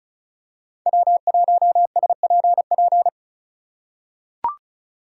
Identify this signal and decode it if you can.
Morse code, heard as a keyed tone, about 34 wpm, 700 Hz. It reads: W1SPP
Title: USA Call Signs - 2x Character Spacing - 35wpm